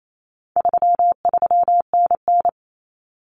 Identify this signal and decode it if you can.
Morse code, heard as a keyed tone, about 28 wpm, 700 Hz. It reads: 33NN
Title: Common Cut Numbers - 4-Digit 28wpm